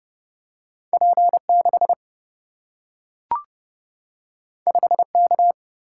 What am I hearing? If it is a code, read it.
Morse code, heard as a keyed tone, about 30 wpm, 700 Hz. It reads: P6 5K
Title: Single-Letter Single-Number 30wpm